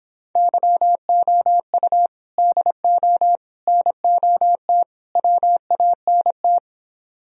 Morse code, heard as a keyed tone, about 29 wpm, 700 Hz. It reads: YOU DO NOT WANT